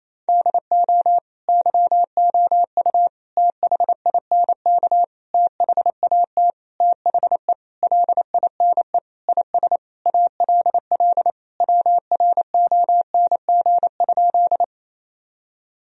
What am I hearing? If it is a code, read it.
Morse code, heard as a keyed tone, about 28 wpm, 700 Hz. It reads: DO YOU THINK THAT THE LINE IS ALL WRONG?
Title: Sentences from Top 1000 Words - Review Part 1 - 28wpm